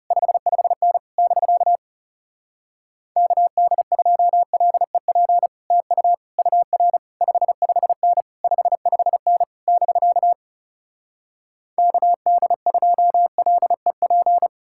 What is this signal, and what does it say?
Morse code, heard as a keyed tone, about 40 wpm, 700 Hz. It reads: 55N <BK> KD2LEP TU UR 55N 55N <BK> KD2LEP